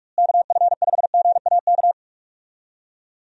Morse code, heard as a keyed tone, about 43 wpm, 700 Hz. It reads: KF5GAK
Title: USA Call Signs - 6 Characters - Repeated 3x - 45wpm